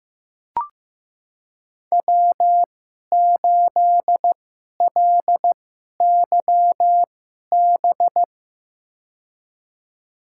Morse code, heard as a keyed tone, about 16 wpm, spaced as for 11 wpm, 700 Hz. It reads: W8LYB
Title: USA Call Signs - 2x Character Spacing - 15wpm